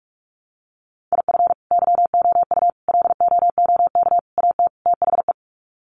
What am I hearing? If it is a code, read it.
Morse code, heard as a keyed tone, about 44 wpm, 700 Hz. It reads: IF YOU LOOK AT THE
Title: Sets of 5 Words 45wpm